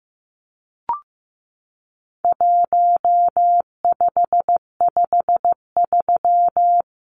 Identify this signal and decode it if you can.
Morse code, heard as a keyed tone, about 15 wpm, 700 Hz. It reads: 1553
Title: Numbers - 4-Digit 15wpm